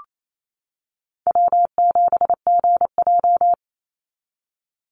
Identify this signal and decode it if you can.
Morse code, heard as a keyed tone, about 28 wpm, 700 Hz. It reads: W7GJ
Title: NA SOTA Chaser Callsigns 28wpm